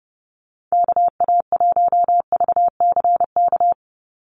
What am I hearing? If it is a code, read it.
Morse code, heard as a keyed tone, about 30 wpm, 700 Hz. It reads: KA1VCK
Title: USA Call Signs 30wpm